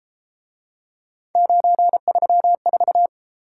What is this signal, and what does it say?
Morse code, heard as a keyed tone, about 34 wpm, 700 Hz. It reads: 934